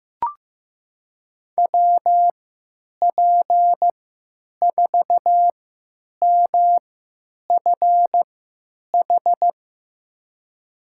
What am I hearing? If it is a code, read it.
Morse code, heard as a keyed tone, about 15 wpm, spaced as for 9 wpm, 700 Hz. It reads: WP4MFH
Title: USA Call Signs - 6 Characters - 3x Character Spacing - 15wpm